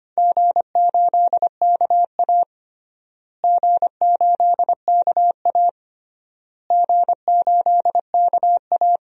Morse code, heard as a keyed tone, about 25 wpm, 700 Hz. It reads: G8KA G8KA G8KA
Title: DXCC-100 Call Signs - 4 Characters - Repeated 3x - 25wpm